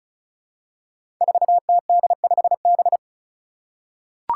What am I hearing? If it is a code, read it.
Morse code, heard as a keyed tone, about 34 wpm, 700 Hz. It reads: 4TD5B
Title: Five Challenging Letter-Number Combinations 35wpm